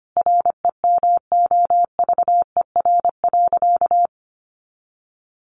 Morse code, heard as a keyed tone, about 25 wpm, 700 Hz. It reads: REMOVER.
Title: Additional Easy Phrases 25wpm